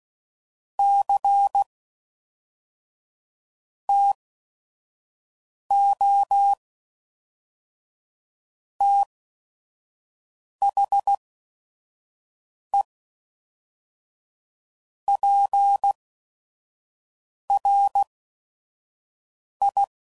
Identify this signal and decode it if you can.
Morse code, heard as a keyed tone, about 16 wpm, spaced as for 5 wpm, 800 Hz. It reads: C TO THE PRI
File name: CW Code Groups @5WPM